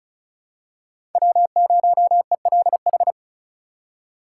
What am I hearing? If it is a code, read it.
Morse code, heard as a keyed tone, about 36 wpm, 700 Hz. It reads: W0ELH
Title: USA Call Signs - 5 Characters - 2x Character Spacing - 35wpm